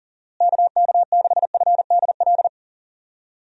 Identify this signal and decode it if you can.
Morse code, heard as a keyed tone, about 40 wpm, 700 Hz. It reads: KK6FDL
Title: USA Call Signs - 6 Characters - Repeated 3x - 40wpm